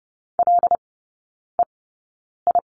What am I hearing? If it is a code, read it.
Morse code, heard as a keyed tone, about 30 wpm, spaced as for 9 wpm, 700 Hz. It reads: LEI
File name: 081 Lesson 14e - Number 2 - Three Characters 30wpm